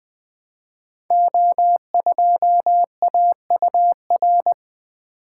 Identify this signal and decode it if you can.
Morse code, heard as a keyed tone, about 20 wpm, 700 Hz. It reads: O2AUR
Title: Code Groups - 5 Letter:Numbers - 20wpm